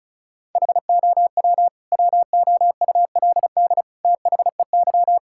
Morse code, heard as a keyed tone, about 34 wpm, 700 Hz. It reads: HOW WOULD THEY
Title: Sentences from Top 1000 Words - Review Part 6 - 35wpm